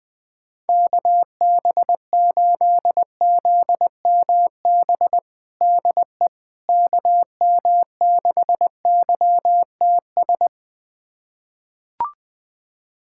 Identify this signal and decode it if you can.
Morse code, heard as a keyed tone, about 20 wpm, 700 Hz. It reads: KB8ZMB DE KM6YTS
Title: USA Callsign DE Callsign - 6 Characters - 20wpm